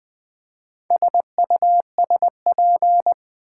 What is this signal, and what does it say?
Morse code, heard as a keyed tone, about 20 wpm, 700 Hz. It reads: SUSP
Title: Build A Word - 4 Letters At A Time - 10-Letter Words - 20wpm